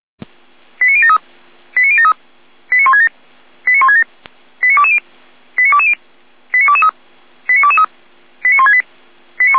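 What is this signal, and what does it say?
Five-tone selective call (ZVEI), heard as quick runs of five tones, tones of about 80 ms. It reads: 89073 89073 89167 89167 89220 89220 89223 89223 89168
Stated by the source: ZVEI five-tone call received by SDR.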